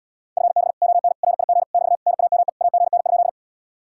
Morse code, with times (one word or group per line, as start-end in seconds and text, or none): 0.4-0.7 FB
0.8-1.1 ON
1.2-1.6 RIG
1.7-2.0 <BT>
2.1-2.5 NAME
2.6-3.3 AGN?